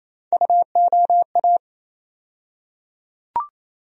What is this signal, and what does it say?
Morse code, heard as a keyed tone, about 28 wpm, 700 Hz. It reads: UOA
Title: USA Call Sign Postfixes - 3 Letters 28wpm